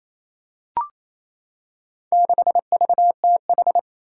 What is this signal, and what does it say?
Morse code, heard as a keyed tone, about 28 wpm, 700 Hz. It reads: 6VTH
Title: Four Challenging Letter-Number Combinations 28wpm